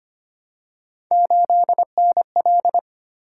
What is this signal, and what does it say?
Morse code, heard as a keyed tone, about 25 wpm, 700 Hz. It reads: 8NL
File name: Three Letter-Number Combinations - 25wpm